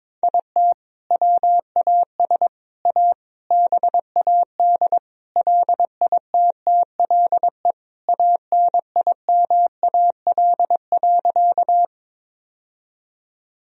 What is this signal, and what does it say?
Morse code, heard as a keyed tone, about 22 wpm, 700 Hz. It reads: IT WAS A BAD LITTLE ANIMAL.
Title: Sentences from Top 1000 Words - Review Part 1 - 22wpm